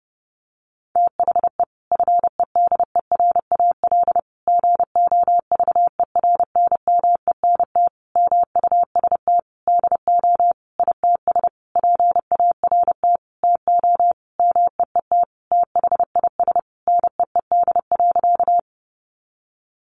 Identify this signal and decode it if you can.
Morse code, heard as a keyed tone, about 30 wpm, 700 Hz. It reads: THE FEDERAL GOVERNMENT MUST DO ITS PART TO MEET THIS NEED.
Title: Sentences from Top 1100 Words 30wpm